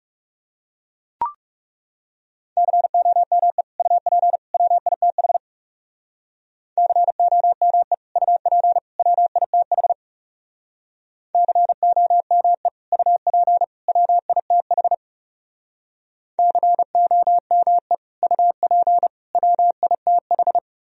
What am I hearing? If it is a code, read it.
Morse code, heard as a keyed tone, about 38 wpm, 700 Hz. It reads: COME UP WITH COME UP WITH COME UP WITH COME UP WITH